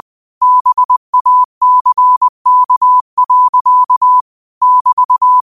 Morse code, heard as a keyed tone, about 20 wpm, 1000 Hz. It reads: BACK. <BT>
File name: qotdfiles / 20wpm